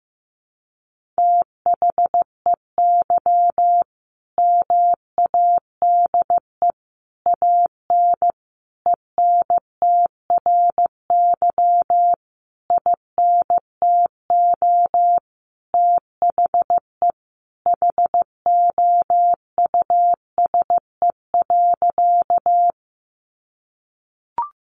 Morse code, heard as a keyed tone, about 15 wpm, 700 Hz. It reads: THEY MADE AN ENTRY INTO THE HOUSE.
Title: Sentences from Top 1500 Words 15wpm